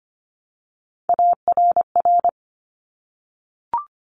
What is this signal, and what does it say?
Morse code, heard as a keyed tone, about 25 wpm, 700 Hz. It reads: ARR
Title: Top Three Letter Combinations in Words 25wpm